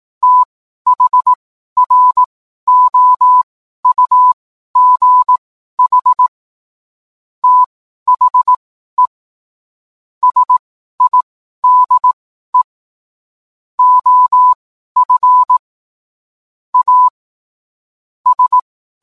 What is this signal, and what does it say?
Morse code, heard as a keyed tone, about 18 wpm, spaced as for 12 wpm, 1000 Hz. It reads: THROUGH THE SIDE OF A S